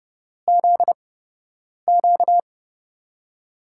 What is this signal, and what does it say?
Morse code, heard as a keyed tone, about 29 wpm, spaced as for 8 wpm, 700 Hz. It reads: ZQ
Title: USA Call Signs - 5 Characters - 8x Character Spacing - 30wpm